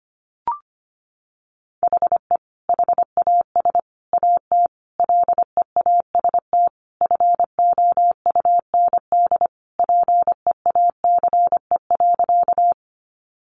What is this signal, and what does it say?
Morse code, heard as a keyed tone, about 25 wpm, 700 Hz. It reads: HE HAS AT LEAST FOUND PEACE.
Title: Sentences from Top 1500 Words 25wpm